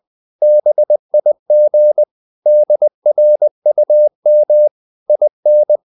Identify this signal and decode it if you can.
Morse code, heard as a keyed tone, about 20 wpm, 600 Hz. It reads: BIG DRUM IN
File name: the old man and the sea / the old man and the sea part 1 20wpm